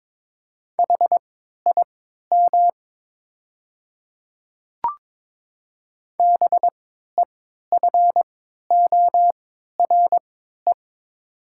Morse code, heard as a keyed tone, about 22 wpm, spaced as for 12 wpm, 700 Hz. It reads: HIM BEFORE
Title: 3x Character Spacing - Top 200 Words - 22wpm